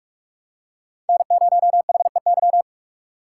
Morse code, heard as a keyed tone, about 42 wpm, 700 Hz. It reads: N0HEY
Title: USA Call Signs - 5 Characters - 3x Character Spacing - 45wpm